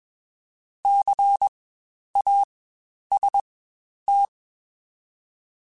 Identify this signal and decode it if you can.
Morse code, heard as a keyed tone, about 21 wpm, spaced as for 10 wpm, 800 Hz. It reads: CAST
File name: Spaced Alphas / Spaced Alphas mp3s / Cs sp @20